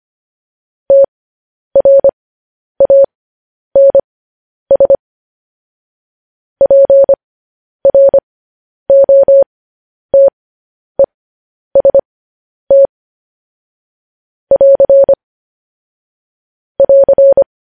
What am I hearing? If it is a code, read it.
Morse code, heard as a keyed tone, about 25 wpm, spaced as for 10 wpm, 550 Hz. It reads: TRANS PROTEST <AR> <AR>